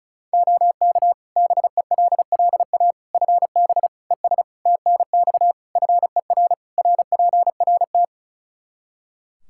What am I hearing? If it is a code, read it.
Morse code, heard as a keyed tone, about 36 wpm, 700 Hz. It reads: OK BELLA FB ES TNX FER RPRT
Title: K1USN SST QSO Practice 35wpm